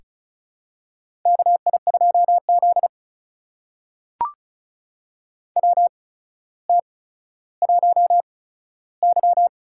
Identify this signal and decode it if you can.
Morse code, heard as a keyed tone, about 36 wpm, spaced as for 15 wpm, 700 Hz. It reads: KI2Z WT1Y
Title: USA Call Signs - 4 Characters - 8x Character Spacing - 35wpm